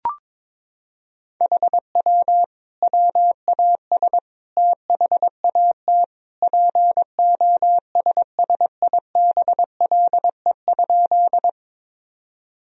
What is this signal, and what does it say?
Morse code, heard as a keyed tone, about 22 wpm, 700 Hz. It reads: HW WAS THAT POSSIBLE?
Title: Sentences from Top 500 Words and Abbreviations 22wpm